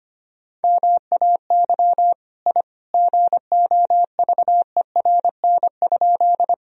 Morse code, heard as a keyed tone, about 25 wpm, 700 Hz. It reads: MAY I GOVERN?
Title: Sentences from Top 1000 Words - Review Part 2 - 25wpm